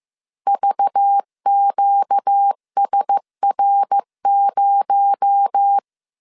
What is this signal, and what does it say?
Morse code, heard as a keyed tone, about 15 wpm, 800 Hz. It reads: VQSR0
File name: CW / DET / 1 6 1500C S C 14wpm 2951